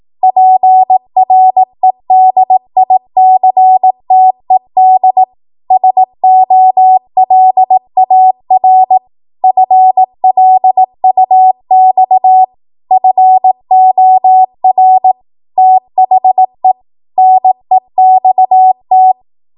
Morse code, heard as a keyed tone, about 18 wpm, 750 Hz. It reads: PREDICTED SOLAR FLUX FOR THE NEXT
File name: Archive / 18 WPM / 18WPM